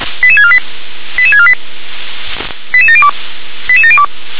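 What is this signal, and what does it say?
Five-tone selective call (ZVEI), heard as quick runs of five tones, tones of about 70 ms. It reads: 88648 88648 88702 88702
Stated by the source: ZVEI five-tone call 88510 received on an SDR.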